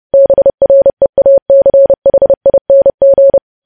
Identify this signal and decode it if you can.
Morse code, heard as a keyed tone, about 30 wpm, 550 Hz. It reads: BREACHING
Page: news headlines in morse code at 30 wpm